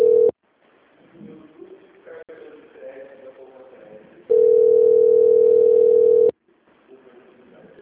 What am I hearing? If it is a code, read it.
Ringback tone (North America), 440 + 480 Hz, with speech.
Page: Ourinhos - SP